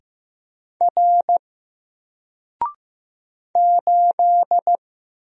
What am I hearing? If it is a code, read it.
Morse code, heard as a keyed tone, about 15 wpm, 700 Hz. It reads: R 8
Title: Single Letter-Number Alt 15wpm